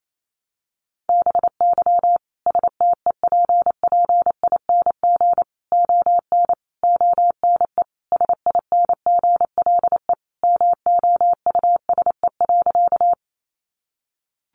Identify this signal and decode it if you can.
Morse code, heard as a keyed tone, about 28 wpm, 700 Hz. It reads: BY STEPPING ON ONE SINGLE MOUSE.
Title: A Sound Of Thunder 28wpm